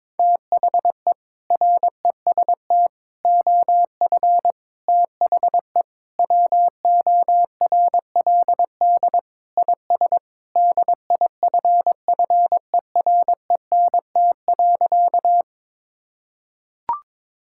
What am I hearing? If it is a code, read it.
Morse code, heard as a keyed tone, about 22 wpm, 700 Hz. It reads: THE REST OF THE WORLD IS DIFFERENT.
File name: practice / 22wpm / Sentences with 4-Word Phrases - 22wpm